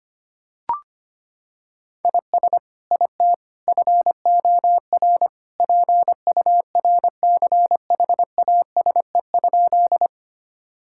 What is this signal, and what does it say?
Morse code, heard as a keyed tone, about 25 wpm, 700 Hz. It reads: IS IT FOR PURCHASE?
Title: Sentences from Top 1000 Words - Review Part 5 - 25wpm